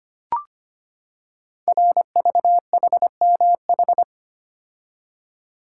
Morse code, heard as a keyed tone, about 25 wpm, 700 Hz. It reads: RVHMH
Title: Code Groups - 5 Letters - 25wpm